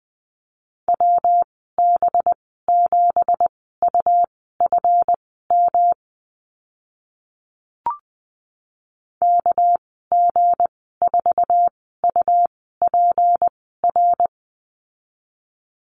Morse code, heard as a keyed tone, about 20 wpm, spaced as for 15 wpm, 700 Hz. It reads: WB7UFM KG4UPR
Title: 231 Lesson 37g - Letter x - Call Signs 20wpm